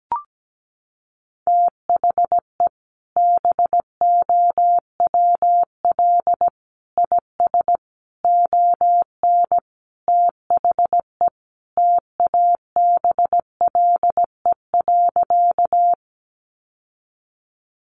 Morse code, heard as a keyed tone, about 17 wpm, 700 Hz. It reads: THE BOWL IS ON THE TABLE.
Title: Sentences from Top 1200 Words 17wpm